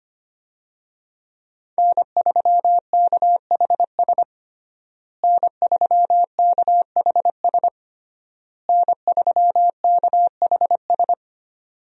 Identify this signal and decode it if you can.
Morse code, heard as a keyed tone, about 25 wpm, 700 Hz. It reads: N3KHS N3KHS N3KHS
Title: USA Call Signs - 5 Characters - Repeated 3x - 25wpm